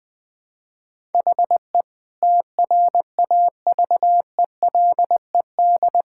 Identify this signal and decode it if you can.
Morse code, heard as a keyed tone, about 20 wpm, 700 Hz. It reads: HE TRAVELED
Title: Sentences with 3-Word Phrases - 20wpm